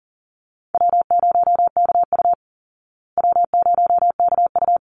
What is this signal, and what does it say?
Morse code, heard as a keyed tone, about 39 wpm, 700 Hz. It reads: W0KU W0KU
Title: Easy Call Signs Repeated 2x - 40wpm